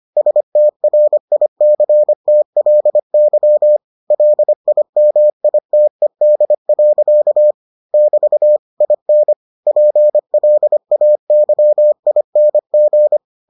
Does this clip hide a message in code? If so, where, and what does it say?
Yes: Morse code from 0.2 to 13.2 s, heard as a keyed tone, about 25 wpm, 600 Hz. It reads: STRICTLY LIMITED. <BT> IN PLAYING